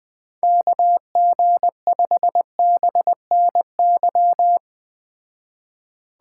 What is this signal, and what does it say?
Morse code, heard as a keyed tone, about 19 wpm, 700 Hz. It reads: KG5BNY
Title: Speed Racing - USA 2x3 Call Signs 17wpm